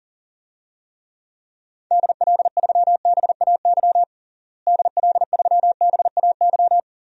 Morse code, heard as a keyed tone, about 40 wpm, 700 Hz. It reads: DL3BAY DL3BAY
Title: DXCC-100 Call Signs - 6 Characters - Repeated 3x - 40wpm